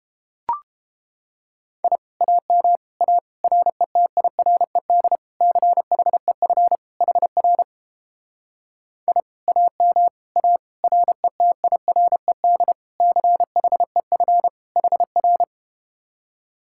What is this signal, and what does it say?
Morse code, heard as a keyed tone, about 32 wpm, 700 Hz. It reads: I AM A RETIRED CHEF HR I AM A RETIRED CHEF HR